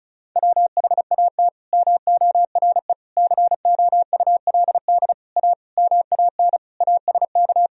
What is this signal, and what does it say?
Morse code, heard as a keyed tone, about 34 wpm, 700 Hz. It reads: WHAT MORE COULD A MAN ASK
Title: Sentences from Top 200 Words and Abbreviations 35wpm